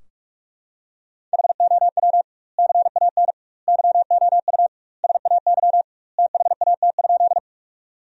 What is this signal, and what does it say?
Morse code, heard as a keyed tone, about 46 wpm, 700 Hz. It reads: HOW CAN YOU SAY THAT?
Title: 2x Word-Spacing - Sentences from Top 200 Words - 45wpm